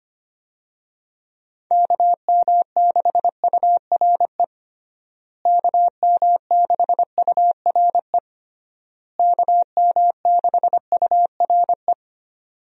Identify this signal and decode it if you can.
Morse code, heard as a keyed tone, about 25 wpm, 700 Hz. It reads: KM6URE KM6URE KM6URE
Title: USA Call Signs - 6 Characters - Repeated 3x - 25wpm